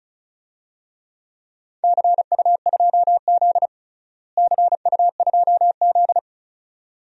Morse code, heard as a keyed tone, about 34 wpm, 700 Hz. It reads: CU2Z CU2Z
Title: DXCC-100 Call Signs - 4 Characters - Repeated 2x - 35wpm